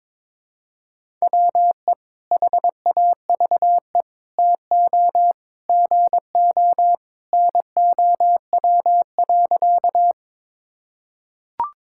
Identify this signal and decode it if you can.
Morse code, heard as a keyed tone, about 22 wpm, 700 Hz. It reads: WE HAVE TO GO NOW.
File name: practice / 22wpm / Sentences with 3-Word Phrases - 22wpm